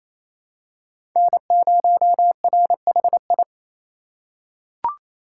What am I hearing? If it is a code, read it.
Morse code, heard as a keyed tone, about 28 wpm, 700 Hz. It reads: N0RHI
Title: USA Call Signs - 3x Character Spacing - 28wpm